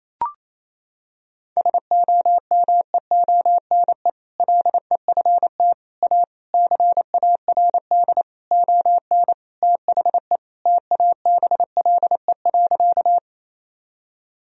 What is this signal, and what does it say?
Morse code, heard as a keyed tone, about 28 wpm, 700 Hz. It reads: SOMEONE LEFT A CARD ON THE TABLE.
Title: Sentences from Top 1000 Words - Review Part 4 - 28wpm